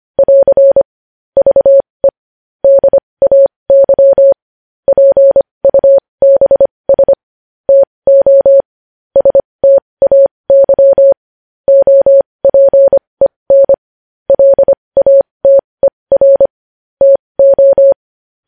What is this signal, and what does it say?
Morse code, heard as a keyed tone, about 25 wpm, spaced as for 20 wpm, 550 Hz. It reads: <AR> VE DAY PUBS TO STAY OPEN LATER TO
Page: news-headlines-in-morse-code-at-20-wpm